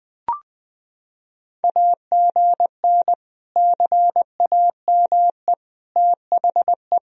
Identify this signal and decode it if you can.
Morse code, heard as a keyed tone, about 20 wpm, 700 Hz. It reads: AGN CAME THE
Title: Sentences from Top 500 Words and Abbreviations 20wpm